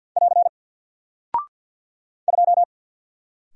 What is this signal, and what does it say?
Morse code, heard as a keyed tone, about 48 wpm, 700 Hz. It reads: <AR> 2
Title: Single Character (Letter:Number:Punctuation:Prosign) Alt 50wpm